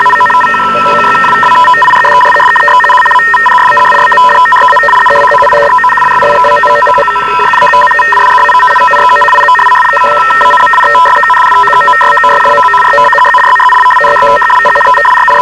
Two-tone teleprinter signal, about 850 Hz shift.